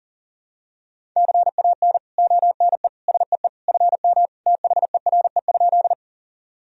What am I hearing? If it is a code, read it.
Morse code, heard as a keyed tone, about 40 wpm, 700 Hz. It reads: CAN ONE SEE FM THERE?
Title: Sentences from Top 100 Words and Abbreviations 40wpm